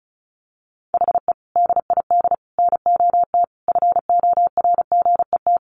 Morse code, heard as a keyed tone, about 34 wpm, 700 Hz. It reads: HE DID NOT FORGET
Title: Sentences from Top 1200 Words 35wpm